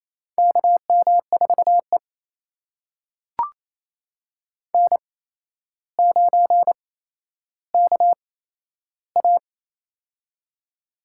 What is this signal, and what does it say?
Morse code, heard as a keyed tone, about 28 wpm, spaced as for 12 wpm, 700 Hz. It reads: KM4E N9KA
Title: USA Call Signs - 4 Characters - 8x Character Spacing - 28wpm